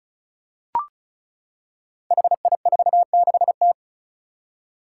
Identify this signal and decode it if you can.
Morse code, heard as a keyed tone, about 36 wpm, 700 Hz. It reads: HI46T
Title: Five Challenging Letter-Number Combinations 35wpm